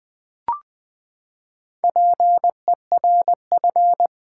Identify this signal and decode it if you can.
Morse code, heard as a keyed tone, about 20 wpm, 700 Hz. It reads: PERF
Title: Build A Word - 4 Letters At A Time - 11-Letter Words - 20wpm